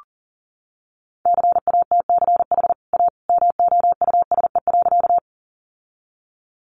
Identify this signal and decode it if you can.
Morse code, heard as a keyed tone, about 40 wpm, 700 Hz. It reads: CATCH A MOUSE.
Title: Sentences from Top 1500 Words 40wpm